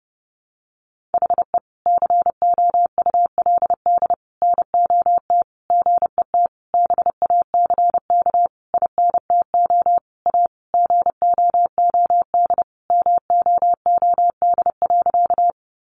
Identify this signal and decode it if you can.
Morse code, heard as a keyed tone, about 28 wpm, 700 Hz. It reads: HE COULD NOT GET BACK INTO A GOOD MOOD.
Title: Sentences from Top 1100 Words 30wpm